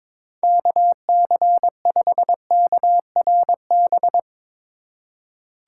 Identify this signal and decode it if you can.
Morse code, heard as a keyed tone, about 21 wpm, 700 Hz. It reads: KC5KRB
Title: USA Call Signs - 3x Character Spacing - 22wpm